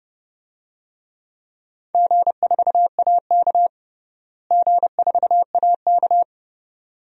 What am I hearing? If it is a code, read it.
Morse code, heard as a keyed tone, about 30 wpm, 700 Hz. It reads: G4AK G4AK
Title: DXCC-100 Call Signs - 4 Characters - Repeated 2x - 30wpm